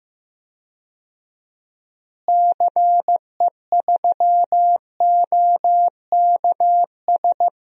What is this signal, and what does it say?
Morse code, heard as a keyed tone, about 15 wpm, 700 Hz. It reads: CE3OKS
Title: DXCC-100 Call Signs - 6 Characters - Repeated 2x - 15wpm